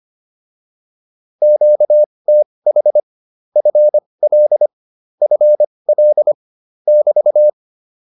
Morse code, heard as a keyed tone, about 25 wpm, spaced as for 20 wpm, 600 Hz. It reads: QTH FL FL <BT>